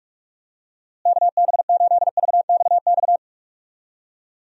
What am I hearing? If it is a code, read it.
Morse code, heard as a keyed tone, about 46 wpm, 700 Hz. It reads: KB8VXX
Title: USA Call Signs - 6 Characters - Repeated 2x - 45wpm